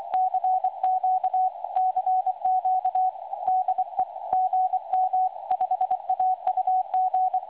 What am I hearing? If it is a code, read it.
Morse code, heard as a keyed tone, about 23 wpm, 750 Hz. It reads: CQ CQ DE GM5AUG
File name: websdr recording 2023-04-10t12-24-59z 5260.0khz